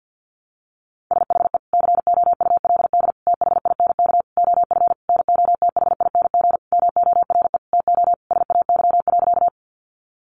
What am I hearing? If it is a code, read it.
Morse code, heard as a keyed tone, about 50 wpm, 700 Hz. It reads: SHE COULD THINK OF NOTHING MORE TO SAY.
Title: Sentences from Top 500 Words 50wpm